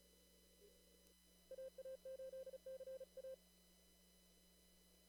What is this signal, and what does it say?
Morse code, heard as a keyed tone, about 37 wpm, 530 Hz. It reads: AA8CA
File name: CW / kt5j wpx cw-op-k5pi